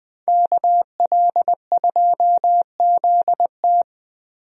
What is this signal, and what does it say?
Morse code, heard as a keyed tone, about 20 wpm, 700 Hz. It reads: KL2ZT
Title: USA Call Signs - 5 Characters - Repeated 2x - 20wpm